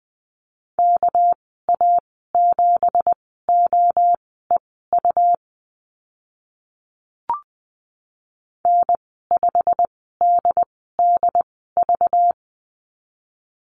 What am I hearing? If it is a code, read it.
Morse code, heard as a keyed tone, about 20 wpm, spaced as for 14 wpm, 700 Hz. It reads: KA7OEU N5DDV
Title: 238 Lesson 38g - Letter q - Call Signs 20wpm